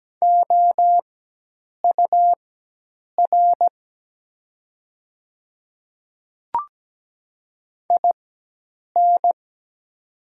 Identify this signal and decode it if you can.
Morse code, heard as a keyed tone, about 17 wpm, spaced as for 8 wpm, 700 Hz. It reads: OUR IN
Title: 4x Character Spacing - Top 300 Words - 17wpm